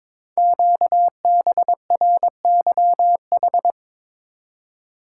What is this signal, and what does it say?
Morse code, heard as a keyed tone, about 22 wpm, 700 Hz. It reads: QBRYH
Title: Code Groups - 5 Letters - 22wpm